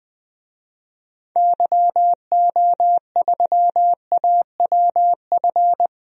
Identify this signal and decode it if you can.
Morse code, heard as a keyed tone, about 21 wpm, 700 Hz. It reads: YO3AWF
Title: DXCC-100 Call Signs - 6 Characters - 20wpm